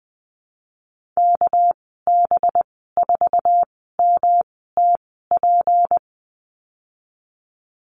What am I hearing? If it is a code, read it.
Morse code, heard as a keyed tone, about 19 wpm, spaced as for 14 wpm, 700 Hz. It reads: KB4MTP
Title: 207 Lesson 33g - Letter j - Call Signs 20wpm